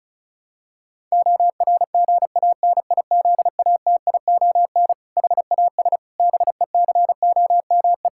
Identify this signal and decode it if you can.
Morse code, heard as a keyed tone, about 36 wpm, 700 Hz. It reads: ORGANIZATION HAS BECOME
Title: Sentences from Top 1000 Words - Review Part 5 - 35wpm